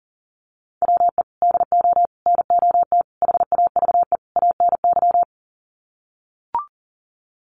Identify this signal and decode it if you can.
Morse code, heard as a keyed tone, about 40 wpm, 700 Hz. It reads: WE DO NOT HAVE ANY
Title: Sets of 5 Words 40wpm